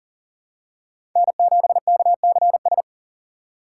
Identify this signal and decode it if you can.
Morse code, heard as a keyed tone, about 40 wpm, 700 Hz. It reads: N7KCS
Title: USA Call Signs - 5 Characters - 2x Character Spacing - 40wpm